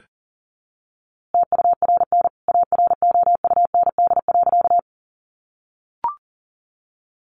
Morse code, heard as a keyed tone, about 40 wpm, 700 Hz. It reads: TURN AROUND.
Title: A Sound Of Thunder 40wpm